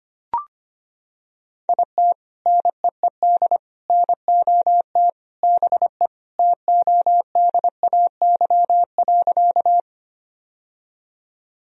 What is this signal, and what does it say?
Morse code, heard as a keyed tone, about 25 wpm, 700 Hz. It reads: IT NEED NOT BE TODAY.
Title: Sentences from Top 1000 Words - Review Part 4 - 25wpm